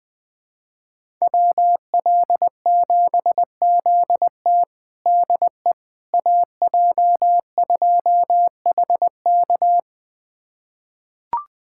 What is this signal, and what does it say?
Morse code, heard as a keyed tone, about 21 wpm, 700 Hz. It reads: WL7ZT DE AJ2HK
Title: USA Callsign DE Callsign - 5 Characters - 20wpm